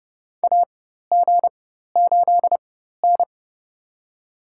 Morse code, heard as a keyed tone, about 30 wpm, spaced as for 14 wpm, 700 Hz. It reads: AG8N
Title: USA Call Signs - 4 Characters - 4x Character Spacing - 30wpm